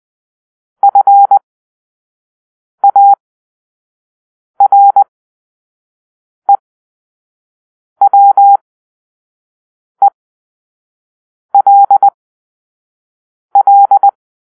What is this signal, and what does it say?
Morse code, heard as a keyed tone, about 20 wpm, spaced as for 5 wpm, 800 Hz. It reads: FAREWELL